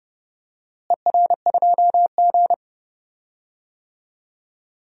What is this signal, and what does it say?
Morse code, heard as a keyed tone, about 32 wpm, 700 Hz. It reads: ER2G
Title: DXCC-100 Call Signs - 4 Characters - Repeated 3x - 30wpm